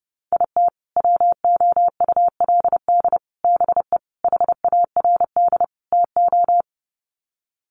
Morse code, heard as a keyed tone, about 30 wpm, 700 Hz. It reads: IT WOULD BE HARD TO
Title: Sets of 5 Words Encore2 30wpm